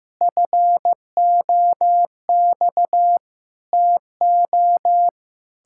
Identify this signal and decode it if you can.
Morse code, heard as a keyed tone, about 15 wpm, 700 Hz. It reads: FOX TO
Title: Pangram Challenge v2 - 15wpm